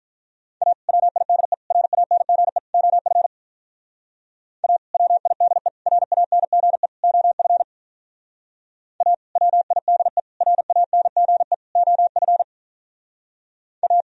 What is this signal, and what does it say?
Morse code, heard as a keyed tone, about 44 wpm, 700 Hz. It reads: A WIDE RANGE OF A WIDE RANGE OF A WIDE RANGE OF A
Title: Speed-Racing - Sets of 4 Words 35wpm